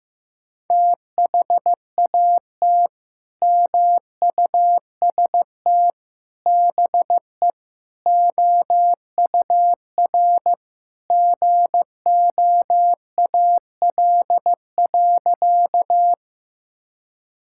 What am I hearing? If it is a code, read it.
Morse code, heard as a keyed tone, about 15 wpm, 700 Hz. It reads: THAT MUST BE OUR GOAL.
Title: Sentences from Top 1000 Words - Review Part 5 - 15wpm